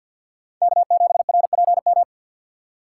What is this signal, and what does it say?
Morse code, heard as a keyed tone, about 52 wpm, 690 Hz. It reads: K7RPM
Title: USA Call Signs - 5 Characters - 6x Character Spacing - 50wpm